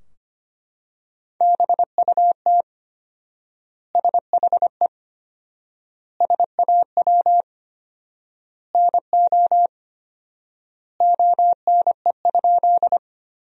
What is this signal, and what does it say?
Morse code, heard as a keyed tone, about 25 wpm, 700 Hz. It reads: BUT SHE SAW NO ONE?
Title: 4x Word-Spacing - Sentences from Top 300 Words - 25wpm